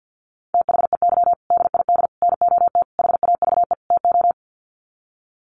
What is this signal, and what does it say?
Morse code, heard as a keyed tone, about 50 wpm, 700 Hz. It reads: THEY DID NOT HAVE TO
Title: Sets of 5 Words Encore2 50wpm